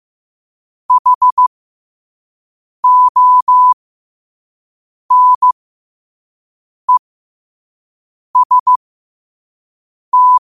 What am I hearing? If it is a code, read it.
Morse code, heard as a keyed tone, about 15 wpm, spaced as for 5 wpm, 1000 Hz. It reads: HONEST
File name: qotdfiles / 5wpm